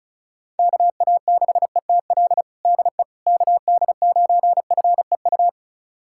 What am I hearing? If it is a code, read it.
Morse code, heard as a keyed tone, about 34 wpm, 700 Hz. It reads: KA6ETL DE KD9FEU
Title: USA Callsign DE Callsign - 6 Characters - 35wpm